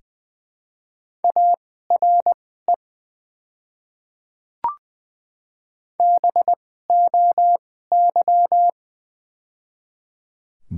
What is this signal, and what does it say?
Morse code, heard as a keyed tone, about 20 wpm, spaced as for 14 wpm, 700 Hz. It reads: ARE BOY
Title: 2x Character Spacing - Top 200 Words - 20wpm